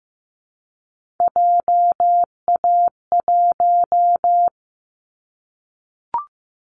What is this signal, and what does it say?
Morse code, heard as a keyed tone, about 15 wpm, 700 Hz. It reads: JA1
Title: Callsign Prefixes for Top 50 DXCC - 15wpm